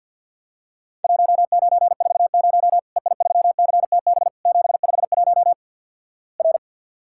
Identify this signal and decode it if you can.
Morse code, heard as a keyed tone, about 50 wpm, 700 Hz. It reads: 1940 EI3CTB 751 R
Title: Worked All Europe DX Contest - QTC Exchange - 50wpm